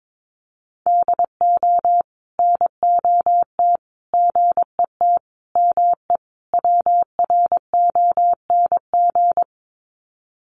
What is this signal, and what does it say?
Morse code, heard as a keyed tone, about 22 wpm, 700 Hz. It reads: DO NOT GET ME WRONG
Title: Sets of 5 Words Encore 22wpm